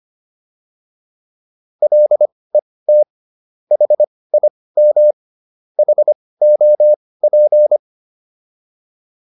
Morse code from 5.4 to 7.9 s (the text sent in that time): HOP